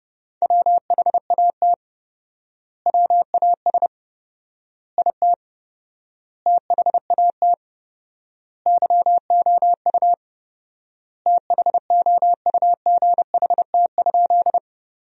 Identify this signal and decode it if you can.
Morse code, heard as a keyed tone, about 30 wpm, 700 Hz. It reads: WHAT WAS IT THAT YOU THOUGHT?
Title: 4x Word-Spacing - Sentences from Top 300 Words - 30wpm